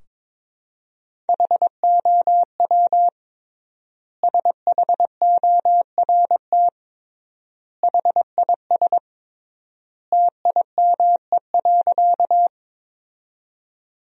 Morse code, heard as a keyed tone, about 21 wpm, 700 Hz. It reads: HOW SHORT HIS TIME.
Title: 3x Word-Spacing - Sentences from Top 400 Words - 22wpm